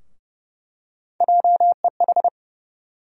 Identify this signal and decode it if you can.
Morse code, heard as a keyed tone, about 30 wpm, 700 Hz. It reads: JEH
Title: USA Call Sign Postfixes - 3 Letters 30wpm